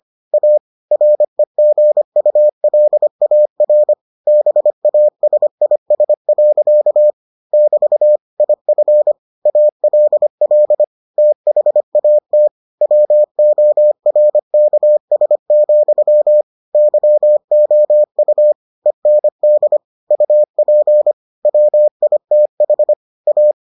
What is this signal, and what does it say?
Morse code, heard as a keyed tone, about 25 wpm, 600 Hz. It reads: A REGULAR BASIS. <BT> IF ALL THAT WORKS, YOU END UP WITH A